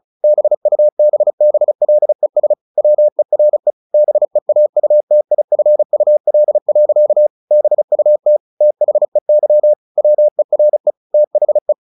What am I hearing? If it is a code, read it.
Morse code, heard as a keyed tone, about 34 wpm, 600 Hz. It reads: BUBBLES WERE BEAUTIFUL. BUT THEY WERE THE
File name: the old man and the sea / the old man and the sea 1 35wpm